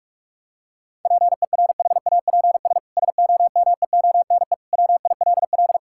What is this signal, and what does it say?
Morse code, heard as a keyed tone, about 46 wpm, 700 Hz. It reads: PERHAPS SOMEONE WILL
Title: Sentences from Top 1600 Words 45wpm